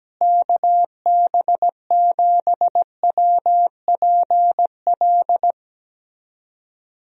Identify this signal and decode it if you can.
Morse code, heard as a keyed tone, about 17 wpm, 700 Hz. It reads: KB7WPL
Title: USA Call Signs - 6 Characters - Repeated 2x - 17wpm